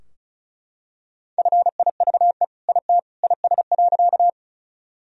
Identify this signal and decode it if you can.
Morse code, heard as a keyed tone, about 36 wpm, 700 Hz. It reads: FIVE IT IS.
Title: Sentences from Top 1000 Words - Review Part 2 - 35wpm